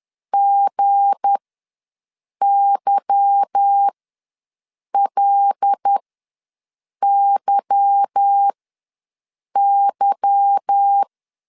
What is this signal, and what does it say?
Morse code, heard as a keyed tone, about 10 wpm, spaced as for 6 wpm, 800 Hz. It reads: GYLYY